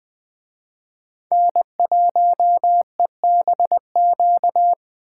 Morse code, heard as a keyed tone, about 20 wpm, 700 Hz. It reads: N1EBQ
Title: Speed Racing - USA 1x3 Call Signs 17wpm